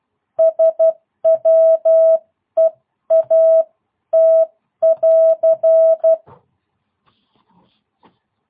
Morse code, heard as a keyed tone, about 12 wpm, spaced as for 10 wpm, 650 Hz. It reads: SWEAT <AR>
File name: jh8cbh / 2022cw